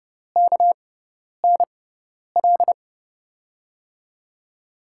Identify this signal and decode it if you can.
Morse code, heard as a keyed tone, about 30 wpm, spaced as for 10 wpm, 700 Hz. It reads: KNL
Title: USA Call Signs - 6 Characters - 6x Character Spacing - 30wpm